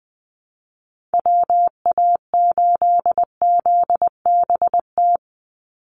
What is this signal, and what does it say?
Morse code, heard as a keyed tone, about 20 wpm, 700 Hz. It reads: WA8ZBT
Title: Call Signs Repeated 3x - 20wpm